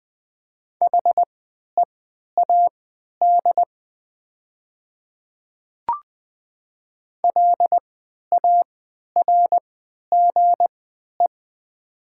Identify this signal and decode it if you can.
Morse code, heard as a keyed tone, about 20 wpm, spaced as for 11 wpm, 700 Hz. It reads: HEAD LARGE
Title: 3x Character Spacing - Top 300 Words - 20wpm